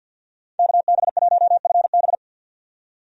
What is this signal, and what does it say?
Morse code, heard as a keyed tone, about 52 wpm, 700 Hz. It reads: KB1FB
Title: USA Call Signs - 5 Characters - 50wpm